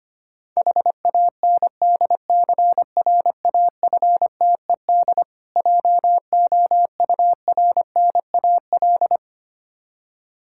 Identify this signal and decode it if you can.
Morse code, heard as a keyed tone, about 25 wpm, 700 Hz. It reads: HANDCRAFTED JOURNAL
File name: practice / 25wpm / Difficult Words with Hints - 25wpm